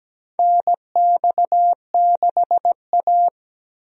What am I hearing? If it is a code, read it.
Morse code, heard as a keyed tone, about 17 wpm, 700 Hz. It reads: NX6A
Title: USA Call Signs - 4 Characters - Repeated 3x - 17wpm